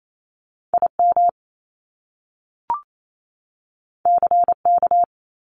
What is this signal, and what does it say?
Morse code, heard as a keyed tone, about 28 wpm, 700 Hz. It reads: IM CK
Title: Top Two Letter Combinations in Words 28wpm